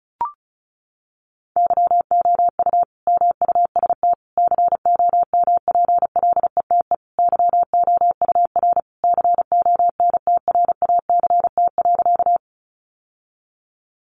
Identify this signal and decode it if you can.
Morse code, heard as a keyed tone, about 34 wpm, 700 Hz. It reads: YOU MUST COMPLETE YOUR CONTRACT.
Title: Sentences from Top 1400 Words 35wpm